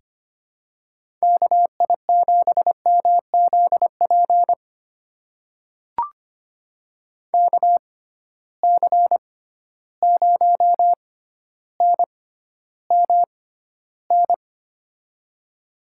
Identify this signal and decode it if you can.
Morse code, heard as a keyed tone, about 25 wpm, spaced as for 13 wpm, 700 Hz. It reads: KI7MZP KC0NMN
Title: USA Call Signs - 6 Characters - 6x Character Spacing - 25wpm